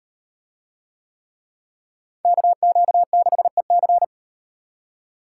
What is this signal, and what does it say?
Morse code, heard as a keyed tone, about 38 wpm, 700 Hz. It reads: KQ6EC